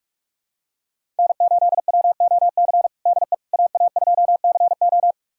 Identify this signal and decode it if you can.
Morse code, heard as a keyed tone, about 46 wpm, 700 Hz. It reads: N8WOC DE AA2CO
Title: USA Callsign DE Callsign - 5 Characters - 45wpm